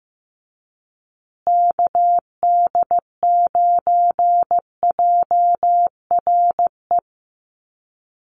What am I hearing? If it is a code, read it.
Morse code, heard as a keyed tone, about 15 wpm, 700 Hz. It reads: KD9JRE
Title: USA Call Signs Repeated 2x 15wpm